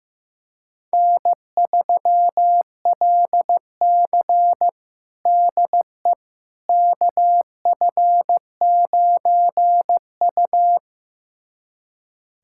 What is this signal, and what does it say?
Morse code, heard as a keyed tone, about 15 wpm, 700 Hz. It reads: N3LC DE KF9U
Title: USA Callsign DE Callsign - 4 Characters - 15wpm